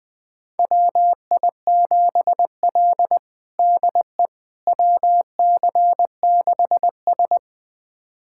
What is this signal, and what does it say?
Morse code, heard as a keyed tone, about 20 wpm, 700 Hz. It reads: WI7L DE WC6S
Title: USA Callsign DE Callsign - 4 Characters - 20wpm